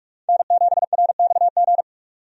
Morse code, heard as a keyed tone, about 46 wpm, 700 Hz. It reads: N7RXG
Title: USA Call Signs - 5 Characters - 45wpm